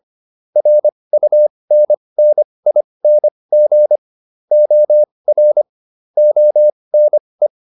Morse code, heard as a keyed tone, about 25 wpm, spaced as for 20 wpm, 600 Hz. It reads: RUNNING OR ONE